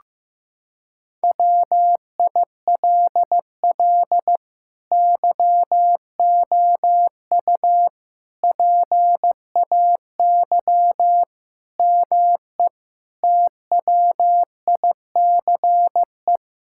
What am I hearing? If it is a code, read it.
Morse code, heard as a keyed tone, about 15 wpm, 700 Hz. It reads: WILL YOU PAY ME TWICE
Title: Sentences from Top 1600 Words 15wpm